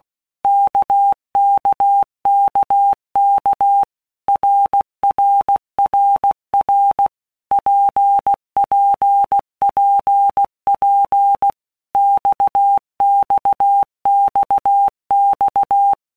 Morse code, heard as a keyed tone, about 16 wpm, 800 Hz. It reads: KKKK RRRR PPPP XXXX
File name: Basic Letter Groups By Blocks / Groups@15 / RKPX@15